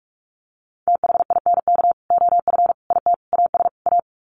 Morse code, heard as a keyed tone, about 44 wpm, 700 Hz. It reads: THINK OF IT AS A
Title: Sets of 5 Words Encore2 45wpm